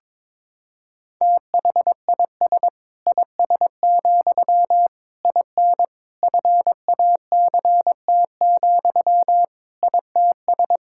Morse code, heard as a keyed tone, about 21 wpm, 700 Hz. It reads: THIS IS, IN FACT, ITS
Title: Sentences from Top 1000 Words - Review Part 5 - 22wpm